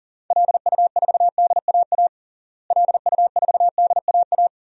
Morse code, heard as a keyed tone, about 40 wpm, 700 Hz. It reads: LU4DAA LU4DAA
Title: DXCC-100 Call Signs - 6 Characters - Repeated 3x - 40wpm